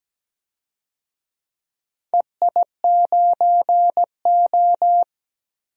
Morse code, heard as a keyed tone, about 17 wpm, 700 Hz. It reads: EI9O
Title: DXCC-100 Call Signs - 4 Characters - Repeated 2x - 17wpm